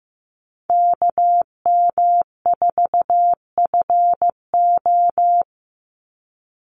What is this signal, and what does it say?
Morse code, heard as a keyed tone, about 15 wpm, 700 Hz. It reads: KM4FO
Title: Call Signs Repeated 2x - 15wpm